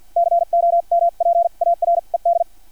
Morse code, heard as a keyed tone, about 50 wpm, spaced as for 41 wpm, 680 Hz. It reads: COMPARED
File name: archived / CRSnet